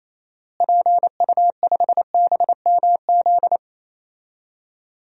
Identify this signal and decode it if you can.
Morse code, heard as a keyed tone, about 28 wpm, 700 Hz. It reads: PU5BMZ
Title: DXCC-100 Call Signs - 6 Characters - Repeated 3x - 28wpm